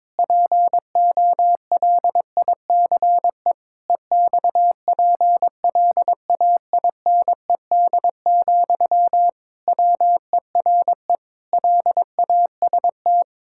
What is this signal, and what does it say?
Morse code, heard as a keyed tone, about 22 wpm, 700 Hz. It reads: POLICE EXPLAINED, WERE LAST
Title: Pangram Challenge v2 - 22wpm